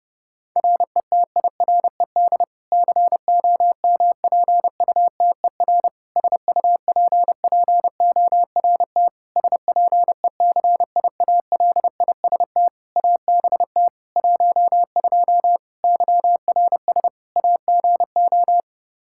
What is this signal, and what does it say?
Morse code, heard as a keyed tone, about 30 wpm, 700 Hz. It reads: RETIRED COMPUTER SUPPORT SPECIALIST ABT 12 YRS AGO
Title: Practice QSO - Exchange 3 - More Info - 30wpm